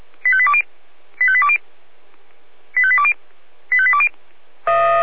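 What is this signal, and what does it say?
Five-tone selective call (ZVEI), heard as quick runs of five tones, tones of about 80 ms. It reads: 85820 85820 85820 85820
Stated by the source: ZVEI five-tone selective call received by SDR.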